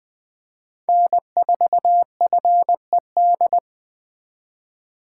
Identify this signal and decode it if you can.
Morse code, heard as a keyed tone, about 24 wpm, 700 Hz. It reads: N4FED
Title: Speed Racing - USA Call Signs 20wpm